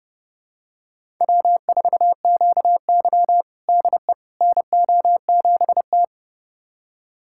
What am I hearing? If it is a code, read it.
Morse code, heard as a keyed tone, about 30 wpm, 700 Hz. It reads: W4QY DE NO7T
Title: USA Callsign DE Callsign - 4 Characters - 30wpm